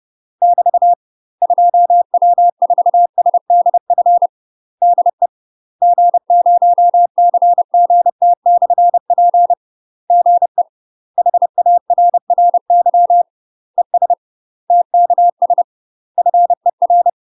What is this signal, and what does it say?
Morse code, heard as a keyed tone, about 30 wpm, 700 Hz. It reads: <BT> 2W4SDF DE G0CGT/P GE HARRY ES TKS FER
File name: UK MASTER QSOs TRAINING / 30WPM / UK MASTER QSOs 0048